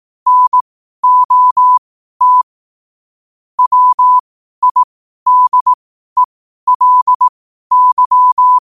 Morse code, heard as a keyed tone, about 18 wpm, spaced as for 12 wpm, 1000 Hz. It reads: NOT WIDELY
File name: qotdfiles / 12wpm